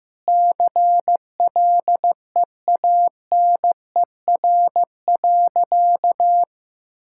Morse code, heard as a keyed tone, about 15 wpm, 700 Hz. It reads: CLEANER.
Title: Pangram Challenge v2 - 15wpm